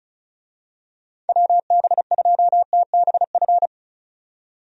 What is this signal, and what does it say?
Morse code, heard as a keyed tone, about 34 wpm, 700 Hz. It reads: WB2TBF
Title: USA Call Signs - 6 Characters - 35wpm